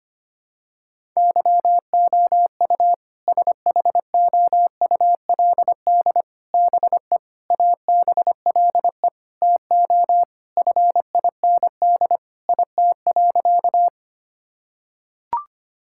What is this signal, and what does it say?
Morse code, heard as a keyed tone, about 25 wpm, 700 Hz. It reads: YOU SHOULD BE ABLE TO FIND IT.
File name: practice / 25wpm / Sentences with 4-Word Phrases - 25wpm